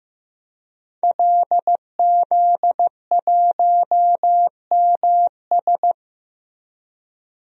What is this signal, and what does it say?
Morse code, heard as a keyed tone, about 15 wpm, 700 Hz. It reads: LZ1MS
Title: DXCC-100 Call Signs - 5 Characters - 15wpm